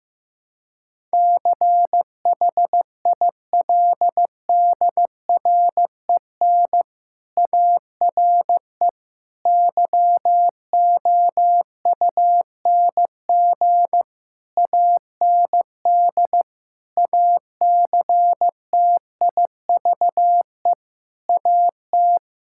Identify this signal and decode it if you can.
Morse code, heard as a keyed tone, about 16 wpm, 700 Hz. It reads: CHILDREN ARE YOUNG AND ACTIVE AT
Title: Sentences from Top 1000 Words - Review Part 6 - 15wpm